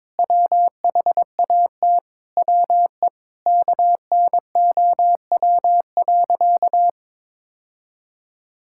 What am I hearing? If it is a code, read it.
Morse code, heard as a keyed tone, about 22 wpm, 700 Hz. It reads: WHAT WE KNOW.
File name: practice / 22wpm / Sentences with 8-Word Phrases - 22wpm